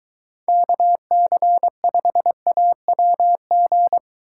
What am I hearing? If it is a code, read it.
Morse code, heard as a keyed tone, about 23 wpm, 700 Hz. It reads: KC5AWG